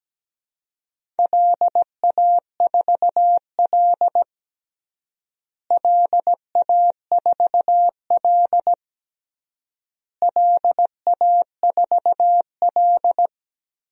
Morse code, heard as a keyed tone, about 17 wpm, 700 Hz. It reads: LA4L LA4L LA4L
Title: DXCC-100 Call Signs - 4 Characters - Repeated 3x - 17wpm